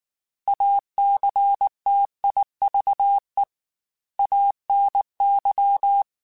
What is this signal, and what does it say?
Morse code, heard as a keyed tone, about 19 wpm, 800 Hz. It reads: ACTIVE ANY
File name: Technical Subjects / Technical Subjects-mp3s / OR Gate @18